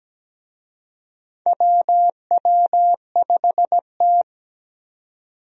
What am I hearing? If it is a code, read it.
Morse code, heard as a keyed tone, about 17 wpm, 700 Hz. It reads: WW5T
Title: USA Call Signs - 4 Characters - 17wpm